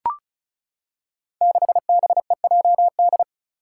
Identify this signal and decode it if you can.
Morse code, heard as a keyed tone, about 36 wpm, 700 Hz. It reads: 6BEJD
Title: Five Challenging Letter-Number Combinations 35wpm